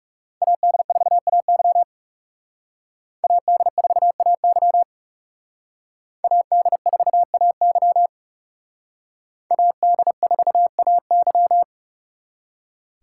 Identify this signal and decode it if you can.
Morse code, heard as a keyed tone, about 36 wpm, 700 Hz. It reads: AD4AY AD4AY AD4AY AD4AY
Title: Speed Racing - USA 2x2 Call Signs 30wpm